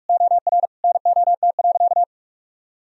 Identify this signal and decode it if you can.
Morse code, heard as a keyed tone, about 46 wpm, 700 Hz. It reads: OR NOT.
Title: Sentences from Top 1000 Words - Review Part 3 - 45wpm